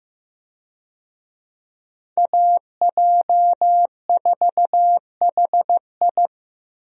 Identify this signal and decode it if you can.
Morse code, heard as a keyed tone, about 15 wpm, 700 Hz. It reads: AJ4HI
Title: USA Call Signs - 5 Characters - Repeated 3x - 15wpm